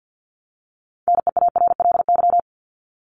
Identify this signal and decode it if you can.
Morse code, heard as a keyed tone, about 50 wpm, 700 Hz. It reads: NEARLY
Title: LY Suffix - Top 100 Words with Base Words 50wpm